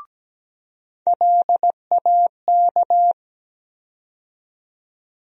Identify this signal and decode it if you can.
Morse code, heard as a keyed tone, about 17 wpm, 700 Hz. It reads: LAK
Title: USA Call Sign Postfixes - 3 Letters 17wpm